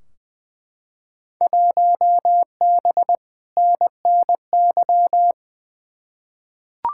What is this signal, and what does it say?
Morse code, heard as a keyed tone, about 20 wpm, 700 Hz. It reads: 1B NNY
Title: Field Day Exchanges - 20wpm